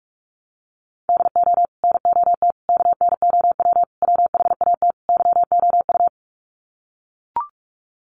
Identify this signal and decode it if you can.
Morse code, heard as a keyed tone, about 44 wpm, 700 Hz. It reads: DO NOT KNOW WHAT YOU
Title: Sets of 5 Words 45wpm